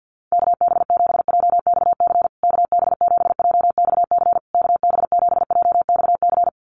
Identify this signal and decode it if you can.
Morse code, heard as a keyed tone, about 50 wpm, 700 Hz. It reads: KB7JXC KB7JXC KB7JXC
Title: USA Call Signs Repeated 3x 50wpm